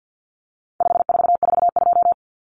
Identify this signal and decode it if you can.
Morse code, heard as a keyed tone, about 50 wpm, 700 Hz. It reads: 5442
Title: Numbers - 4-Digit 50wpm